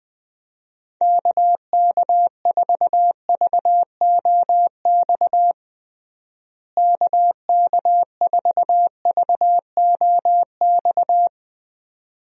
Morse code, heard as a keyed tone, about 21 wpm, 700 Hz. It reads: KK4VOX KK4VOX
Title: USA Call Signs - 6 Characters - Repeated 2x - 20wpm